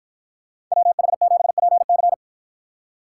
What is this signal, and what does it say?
Morse code, heard as a keyed tone, about 53 wpm, 700 Hz. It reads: WH7PC